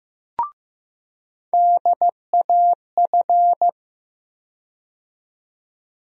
Morse code, heard as a keyed tone, about 15 wpm, 700 Hz. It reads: DAF
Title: USA Call Sign Postfixes - 3 Letters 15wpm